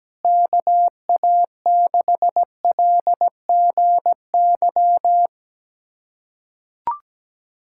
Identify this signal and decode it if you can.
Morse code, heard as a keyed tone, about 17 wpm, 700 Hz. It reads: KA6LGY
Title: USA Callsign DE Callsign 17wpm